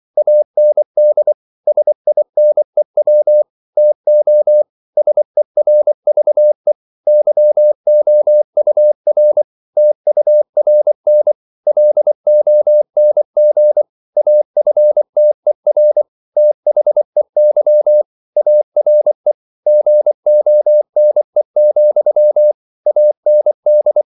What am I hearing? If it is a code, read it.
Morse code, heard as a keyed tone, about 24 wpm, 600 Hz. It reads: AND SINEW TO SERVE YOUR TURN LONG AFTER THEY ARE GONE, AND
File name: CW / Practice / If24